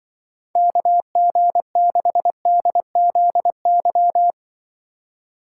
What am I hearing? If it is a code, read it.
Morse code, heard as a keyed tone, about 24 wpm, 700 Hz. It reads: KG6DZY
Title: Speed Racing - USA 2x3 Call Signs 20wpm